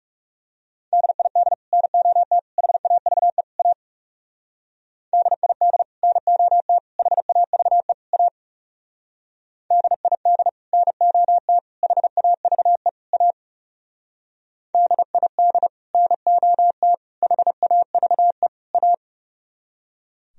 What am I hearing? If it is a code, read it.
Morse code, heard as a keyed tone, about 35 wpm, 700 Hz. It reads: DID NOT HAVE A DID NOT HAVE A DID NOT HAVE A DID NOT HAVE A
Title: Speed-Racing - Sets of 4 Words 30wpm